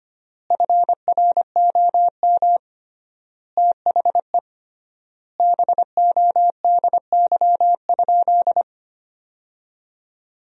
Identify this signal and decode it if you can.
Morse code, heard as a keyed tone, about 25 wpm, 700 Hz. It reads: FROM THE BODY?
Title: 3x Word-Spacing - Sentences from Top 400 Words - 25wpm